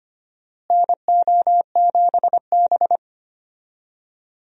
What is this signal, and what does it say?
Morse code, heard as a keyed tone, about 25 wpm, 700 Hz. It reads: NO7B
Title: USA Call Signs - 4 Characters - Repeated 2x - 25wpm